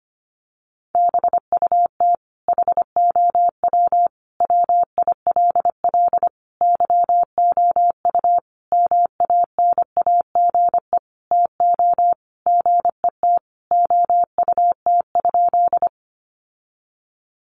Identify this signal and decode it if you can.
Morse code, heard as a keyed tone, about 25 wpm, 700 Hz. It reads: BUT HOW WILL YOU MANAGE TO GET OUT?
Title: Sentences from Top 1400 Words 25wpm